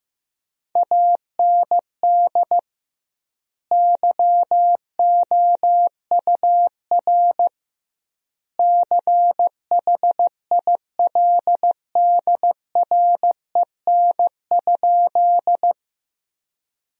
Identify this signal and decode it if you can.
Morse code, heard as a keyed tone, about 15 wpm, 700 Hz. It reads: AND YOUR CHILDREN?
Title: 2x Word-Spacing - Sentences from Top 300 Words - 15wpm